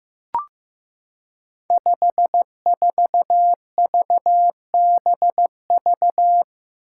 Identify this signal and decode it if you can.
Morse code, heard as a keyed tone, about 15 wpm, 700 Hz. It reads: 54VBV
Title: Five Challenging Letter-Number Combinations 15wpm